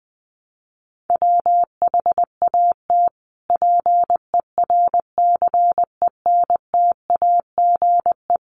Morse code, heard as a keyed tone, about 20 wpm, 700 Hz. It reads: WHAT PERCENTAGE
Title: Sentences from Top 1300 Words 20wpm